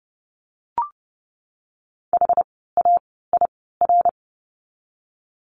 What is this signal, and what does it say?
Morse code, heard as a keyed tone, about 30 wpm, spaced as for 17 wpm, 700 Hz. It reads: HAIR
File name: Lesson 17f - Letter u - Words 30wpm